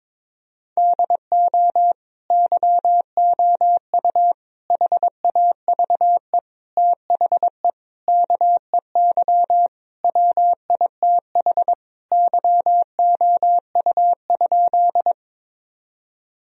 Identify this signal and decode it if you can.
Morse code, heard as a keyed tone, about 23 wpm, 700 Hz. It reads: DO YOU HAVE THE KEY WITH YOU?
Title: Sentences from Top 1000 Words - Review Part 4 - 22wpm